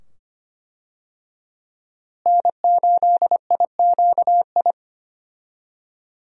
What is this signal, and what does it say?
Morse code, heard as a keyed tone, about 25 wpm, 700 Hz. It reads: N8IQI
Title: USA Call Signs - 5 Characters - Repeated 2x - 25wpm